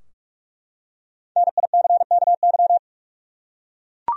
Morse code, heard as a keyed tone, about 46 wpm, 700 Hz. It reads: NICKY
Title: USA Common Names - Max 5 Letters - 45wpm